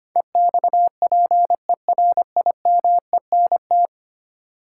Morse code, heard as a keyed tone, about 25 wpm, 700 Hz. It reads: EXPERIMENT
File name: 10 Letter Words - First 3 Letters Given - 25wpm